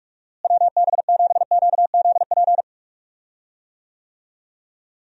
Morse code, heard as a keyed tone, about 46 wpm, 700 Hz. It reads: WB7QZP
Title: USA Call Signs - 6 Characters - Repeated 2x - 45wpm